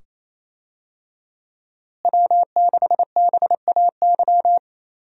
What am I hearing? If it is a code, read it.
Morse code, heard as a keyed tone, about 28 wpm, 700 Hz. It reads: W6BAY
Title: USA Call Signs - 5 Characters - Repeated 2x - 28wpm